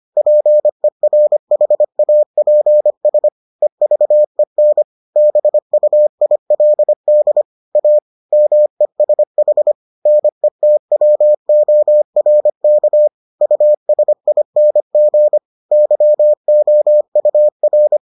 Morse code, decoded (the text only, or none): PERHAPS EVEN BUILD A MESH NETWORK USING YOUR